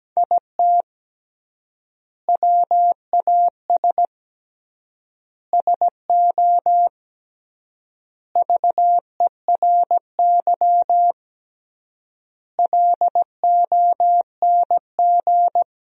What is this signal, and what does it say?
Morse code, heard as a keyed tone, about 17 wpm, 700 Hz. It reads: IT WAS SO VERY LONG
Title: 3x Word-Spacing - Sentences from Top 500 Words - 17wpm